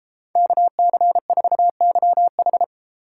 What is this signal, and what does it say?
Morse code, heard as a keyed tone, about 30 wpm, 700 Hz. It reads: KC4YH